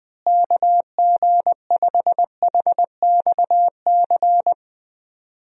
Morse code, heard as a keyed tone, about 20 wpm, 700 Hz. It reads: KG5HXC
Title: USA Call Signs - 6 Characters - Repeated 3x - 20wpm